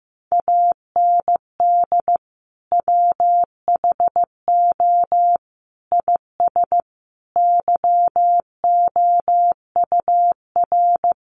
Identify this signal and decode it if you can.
Morse code, heard as a keyed tone, about 15 wpm, 700 Hz. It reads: AND WHO IS YOUR
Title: Sentences from Top 1300 Words 15wpm